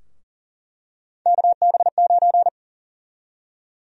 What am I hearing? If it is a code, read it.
Morse code, heard as a keyed tone, about 40 wpm, 700 Hz. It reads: KB9
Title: USA Call Sign Prefixes - Repeated 2x - 40wpm